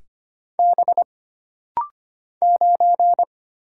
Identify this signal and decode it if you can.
Morse code, heard as a keyed tone, about 25 wpm, 700 Hz. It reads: B 9
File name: Single Character (Letter:Number:Punctuation:Prosign) - Rapid-Fire - 25wpm